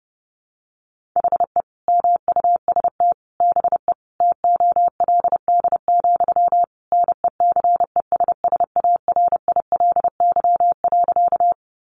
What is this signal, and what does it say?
Morse code, heard as a keyed tone, about 30 wpm, 700 Hz. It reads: HE MUST BE TOLD, NECESSARILY.
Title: Sentences from Top 1400 Words 30wpm